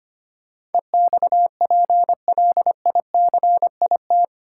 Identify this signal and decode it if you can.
Morse code, heard as a keyed tone, about 25 wpm, 700 Hz. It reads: EXPLICIT
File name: Eight Letter Words - 25wpm